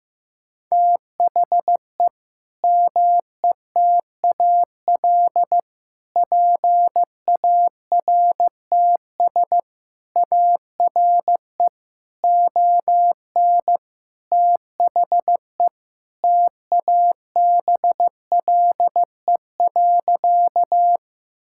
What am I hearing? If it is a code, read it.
Morse code, heard as a keyed tone, about 15 wpm, 700 Hz. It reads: THE METAL PARTS ARE ON THE TABLE.
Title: Sentences from Top 1000 Words - Review Part 6 - 15wpm